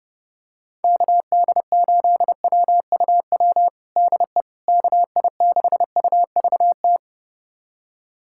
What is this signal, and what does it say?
Morse code, heard as a keyed tone, about 32 wpm, 700 Hz. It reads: KD8WUW DE KI6UVT
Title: USA Callsign DE Callsign - 6 Characters - 30wpm